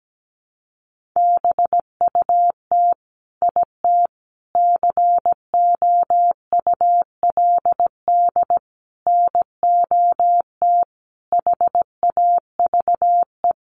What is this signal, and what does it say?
Morse code, heard as a keyed tone, about 17 wpm, 700 Hz. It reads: BUT IT COULD NOT HAVE
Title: Sentences from Top 1400 Words 17wpm